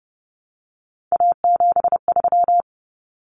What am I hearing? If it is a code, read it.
Morse code, heard as a keyed tone, about 30 wpm, 700 Hz. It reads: A73
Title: Common Cut Numbers - 3-Digit 30wpm